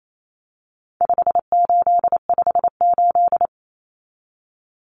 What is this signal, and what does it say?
Morse code, heard as a keyed tone, about 28 wpm, 700 Hz. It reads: 5858
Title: Numbers - 4-Digit 28wpm